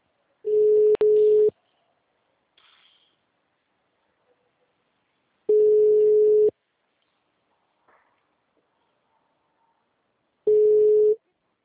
Ringback tone (Europe, 425 Hz).